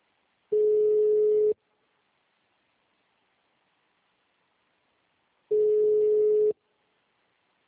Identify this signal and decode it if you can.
Ringback tone (Europe, 425 Hz), with speech.